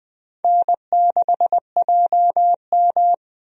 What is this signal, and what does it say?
Morse code, heard as a keyed tone, about 21 wpm, 700 Hz. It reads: N6JM
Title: USA Call Signs - 4 Characters - 20wpm